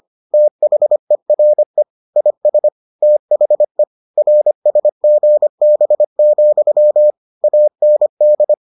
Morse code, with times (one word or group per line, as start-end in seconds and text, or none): 0.3-1.8 THERE
2.2-2.7 IS
3.0-3.8 THE
4.2-7.1 RSGB,
7.4-8.5 AND